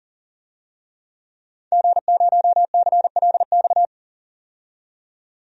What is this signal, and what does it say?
Morse code, heard as a keyed tone, about 39 wpm, 700 Hz. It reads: G0CLX
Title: DXCC-100 Call Signs - 5 Characters - 40wpm